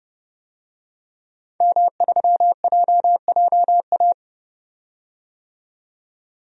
Morse code, heard as a keyed tone, about 32 wpm, 700 Hz. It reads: M3JJA
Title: DXCC-100 Call Signs - 5 Characters - 30wpm